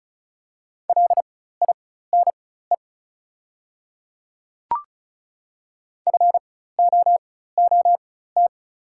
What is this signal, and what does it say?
Morse code, heard as a keyed tone, about 36 wpm, spaced as for 16 wpm, 700 Hz. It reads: LINE FOOT
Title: 4x Character Spacing - Top 500 Words - 35wpm